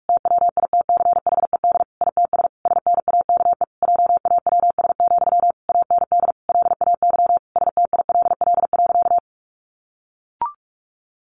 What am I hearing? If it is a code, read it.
Morse code, heard as a keyed tone, about 44 wpm, 700 Hz. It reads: TWITCHED ITS SNAKE JAWS, AND LAY STILL.
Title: A Sound Of Thunder 45wpm